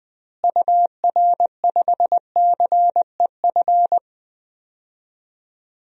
Morse code, heard as a keyed tone, about 20 wpm, 700 Hz. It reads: UR5CEF
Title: DXCC-100 Call Signs - 6 Characters - Repeated 3x - 20wpm